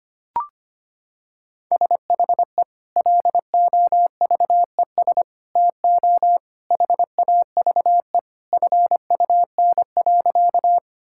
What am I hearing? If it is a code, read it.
Morse code, heard as a keyed tone, about 25 wpm, 700 Hz. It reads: SHE LOVES TO HAVE FUN.
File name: practice / 25wpm / Sentences with 2-Word Phrases - 25wpm